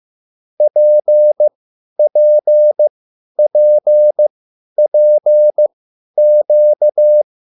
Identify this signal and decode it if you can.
Morse code, heard as a keyed tone, about 15 wpm, spaced as for 10 wpm, 600 Hz. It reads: PPPPQ
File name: QSL / CW lesson